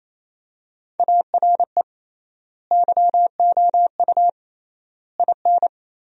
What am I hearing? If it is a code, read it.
Morse code, heard as a keyed tone, about 28 wpm, 700 Hz. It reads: ARE YOU IN
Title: 3x Word-Spacing - Sentences from Top 400 Words - 28wpm